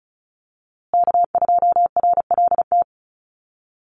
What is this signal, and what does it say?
Morse code, heard as a keyed tone, about 34 wpm, 700 Hz. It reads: K2RLT
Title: USA Call Signs Repeated 2x 35wpm